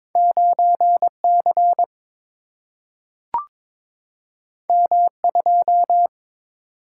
Morse code, heard as a keyed tone, about 22 wpm, 700 Hz. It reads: 9 C M 2
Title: Single-Letter Single-Number 22wpm